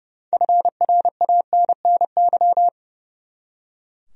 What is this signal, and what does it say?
Morse code, heard as a keyed tone, about 31 wpm, 700 Hz. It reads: FRANNY
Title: USA Common Names - 30wpm